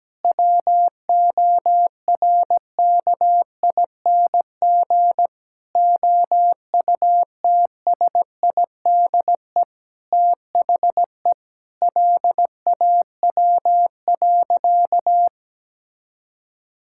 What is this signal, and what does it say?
Morse code, heard as a keyed tone, about 17 wpm, 700 Hz. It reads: WORKING OUTSIDE THE LAW.
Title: Sentences from Top 1000 Words - Review Part 6 - 17wpm